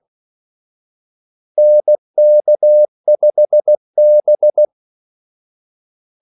Morse code, heard as a keyed tone, about 16 wpm, 600 Hz. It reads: NK5B
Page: Kaldesignaler 16 wpm